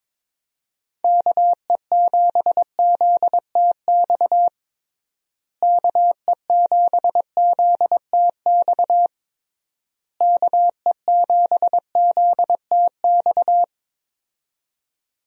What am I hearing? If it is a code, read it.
Morse code, heard as a keyed tone, about 22 wpm, 700 Hz. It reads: KE7ZTX KE7ZTX KE7ZTX
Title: USA Call Signs - 6 Characters - Repeated 3x - 22wpm